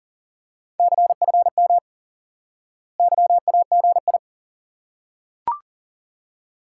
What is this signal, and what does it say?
Morse code, heard as a keyed tone, about 40 wpm, 700 Hz. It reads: CFM YAGI
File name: IQE Course 3 - Lesson 89 - YAGI - Two Elements - 40wpm